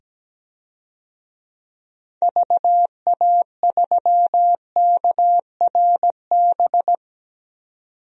Morse code, heard as a keyed tone, about 17 wpm, 700 Hz. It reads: VA3KRB
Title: DXCC-100 Call Signs - 6 Characters - 17wpm